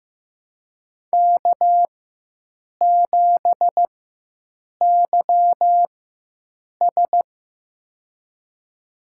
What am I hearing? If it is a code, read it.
Morse code, heard as a keyed tone, about 15 wpm, spaced as for 7 wpm, 700 Hz. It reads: K7YS
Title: USA Call Signs - 4 Characters - 4x Character Spacing - 15wpm